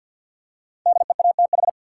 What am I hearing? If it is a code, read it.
Morse code, heard as a keyed tone, about 53 wpm, 700 Hz. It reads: DEATH
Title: Five Letter Words - 50wpm